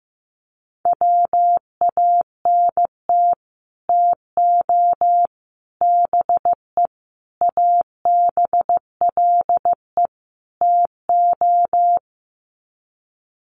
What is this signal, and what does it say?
Morse code, heard as a keyed tone, about 15 wpm, 700 Hz. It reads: WANT TO BE ABLE TO
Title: Sets of 5 Words 15wpm